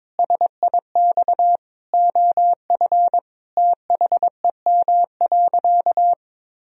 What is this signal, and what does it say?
Morse code, heard as a keyed tone, about 22 wpm, 700 Hz. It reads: SIX OF THEM.
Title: Sentences from Top 1000 Words - Review Part 2 - 22wpm